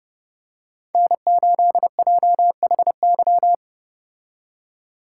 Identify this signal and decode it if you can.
Morse code, heard as a keyed tone, about 29 wpm, 700 Hz. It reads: N8JHY